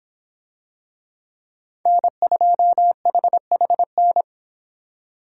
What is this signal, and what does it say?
Morse code, heard as a keyed tone, about 26 wpm, 700 Hz. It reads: N2HHN